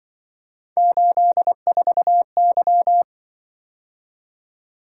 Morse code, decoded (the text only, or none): 84Y